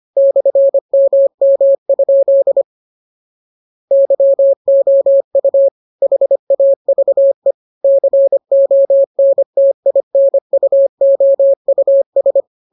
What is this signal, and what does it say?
Morse code, heard as a keyed tone, about 25 wpm, 550 Hz. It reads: /MM? YOU HAVE CONTINUOUS
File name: jokes / joke 9 25 0000